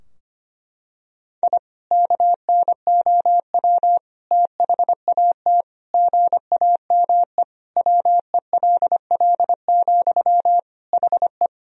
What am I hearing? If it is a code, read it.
Morse code, heard as a keyed tone, about 25 wpm, 700 Hz. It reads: I KNOW THAT GAME WELL, HE
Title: Sentences from Top 1000 Words - Review Part 3 - 25wpm